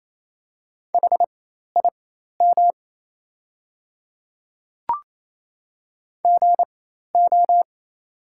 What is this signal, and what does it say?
Morse code, heard as a keyed tone, about 28 wpm, spaced as for 13 wpm, 700 Hz. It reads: HIM GO
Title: 4x Character Spacing - Top 100 Words - 28wpm